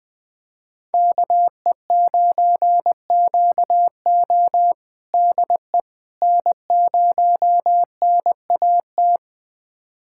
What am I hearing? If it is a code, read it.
Morse code, heard as a keyed tone, about 20 wpm, 700 Hz. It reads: KE9QO DE N0NAT
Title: USA Callsign DE Callsign - 5 Characters - 20wpm